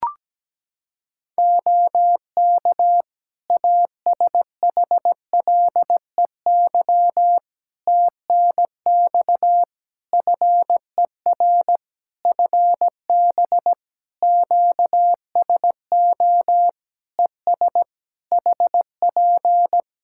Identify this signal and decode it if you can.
Morse code, heard as a keyed tone, about 17 wpm, 700 Hz. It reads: OK ASHLEY TNX FER FB QSO ES HP
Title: Practice QSO - Exchange 4 - Ending - 17wpm